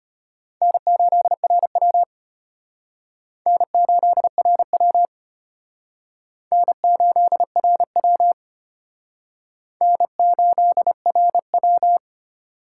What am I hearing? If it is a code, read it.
Morse code, heard as a keyed tone, about 32 wpm, 700 Hz. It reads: N8RW N8RW N8RW N8RW